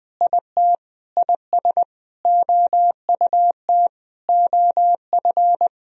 Morse code, heard as a keyed tone, about 20 wpm, 700 Hz. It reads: IT IS OUT OF
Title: Sentences from Top 1000 Words - Review Part 4 - 20wpm